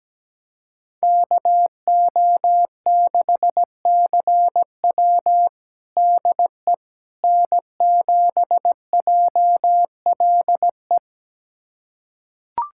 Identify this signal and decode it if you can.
Morse code, heard as a keyed tone, about 17 wpm, 700 Hz. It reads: KO6CW DE N7JLE
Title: USA Callsign DE Callsign - 5 Characters - 17wpm